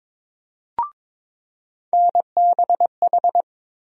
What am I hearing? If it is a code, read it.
Morse code, heard as a keyed tone, about 22 wpm, 700 Hz. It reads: NBH
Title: USA Call Sign Postfixes - 3 Letters 22wpm